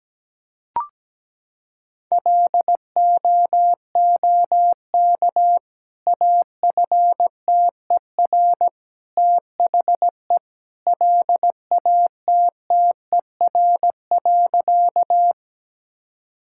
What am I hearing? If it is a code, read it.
Morse code, heard as a keyed tone, about 17 wpm, 700 Hz. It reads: LOOK AFTER THE LATTER.
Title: Sentences from Top 1600 Words 17wpm